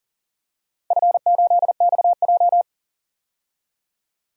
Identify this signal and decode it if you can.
Morse code, heard as a keyed tone, about 40 wpm, 700 Hz. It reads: F8XJ
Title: DXCC-100 Call Signs - 4 Characters - 40wpm